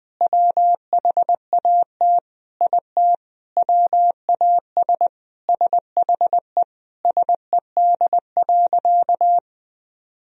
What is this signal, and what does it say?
Morse code, heard as a keyed tone, about 20 wpm, 700 Hz. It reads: WHAT IT WAS SHE SED.
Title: Sentences from Top 400 Words and Abbreviations 20wpm